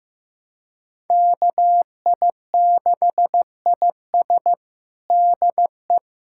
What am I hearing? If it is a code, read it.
Morse code, heard as a keyed tone, about 16 wpm, 700 Hz. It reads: KI6IS DE
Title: USA Callsign DE Callsign - 5 Characters - 15wpm